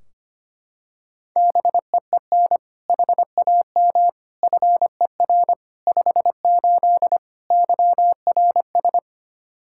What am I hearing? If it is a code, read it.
Morse code, heard as a keyed tone, about 25 wpm, 700 Hz. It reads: BEEN HAM FER 58 YRS
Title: Practice QSO - Exchange 3 - More Info - 25wpm